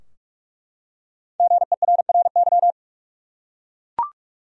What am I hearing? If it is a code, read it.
Morse code, heard as a keyed tone, about 48 wpm, 700 Hz. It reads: GERRY
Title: USA Common Names - Max 5 Letters - 45wpm